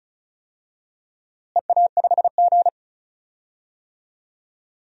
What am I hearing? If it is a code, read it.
Morse code, heard as a keyed tone, about 34 wpm, 700 Hz. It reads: EA5G
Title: DXCC-100 Call Signs - 4 Characters - Repeated 2x - 35wpm